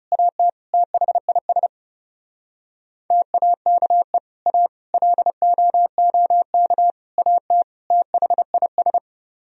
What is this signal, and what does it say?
Morse code, heard as a keyed tone, about 40 wpm, 700 Hz. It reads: AT THIS TAKE A LOOK AT THIS